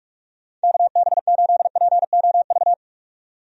Morse code, heard as a keyed tone, about 47 wpm, 700 Hz. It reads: KB8POV
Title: USA Call Signs - 6 Characters - 45wpm